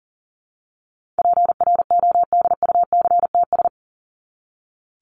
Morse code, heard as a keyed tone, about 40 wpm, 700 Hz. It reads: PRODUCTS
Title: ES:S Suffix - Top 100 Words with Base Words 40wpm